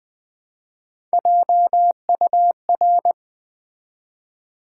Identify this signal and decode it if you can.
Morse code, heard as a keyed tone, about 20 wpm, 700 Hz. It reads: JUR
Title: Build A Word - 1 Letter At A Time - 4-Letter Words - 20wpm